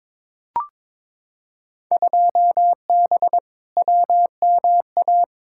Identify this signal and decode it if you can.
Morse code, heard as a keyed tone, about 22 wpm, 700 Hz. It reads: 2B WMA
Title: Field Day Exchanges Encore - 22wpm